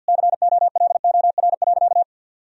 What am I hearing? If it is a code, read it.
Morse code, heard as a keyed tone, about 50 wpm, 700 Hz. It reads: COLOR.
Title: Sentences from Top 1000 Words - Review Part 2 - 50wpm